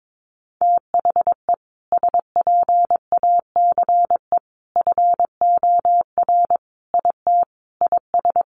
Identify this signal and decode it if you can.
Morse code, heard as a keyed tone, about 22 wpm, 700 Hz. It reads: THE SPACE FOR IT IS
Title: Sentences from Top 1500 Words 22wpm